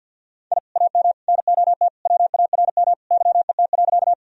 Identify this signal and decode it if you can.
Morse code, heard as a keyed tone, about 50 wpm, 700 Hz. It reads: I AM NOT WARM YET.
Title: Sentences from Top 1000 Words - Review Part 2 - 50wpm